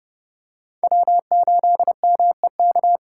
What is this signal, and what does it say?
Morse code, heard as a keyed tone, about 29 wpm, 700 Hz. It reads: W8MEK